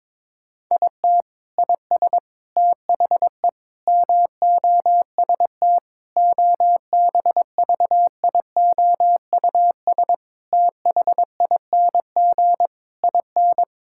Morse code, heard as a keyed tone, about 22 wpm, 700 Hz. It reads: IT IS THE MOST OBVIOUS THING IN
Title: Sentences from Top 1600 Words 22wpm